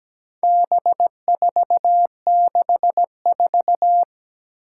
Morse code, heard as a keyed tone, about 17 wpm, 700 Hz. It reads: B464
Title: Four Challenging Letter-Number Combinations 17wpm